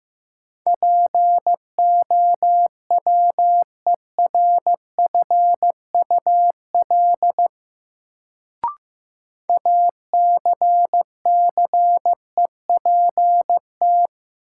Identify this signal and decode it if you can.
Morse code, heard as a keyed tone, about 15 wpm, 700 Hz. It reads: POWERFUL ACCEPT
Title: Sentences from Top 1300 Words - New Word Drill Alt - 15wpm